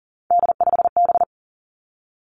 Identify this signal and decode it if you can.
Morse code, heard as a keyed tone, about 42 wpm, 700 Hz. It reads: D5B
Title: Three Challenging Letter-Number Combinations 40wpm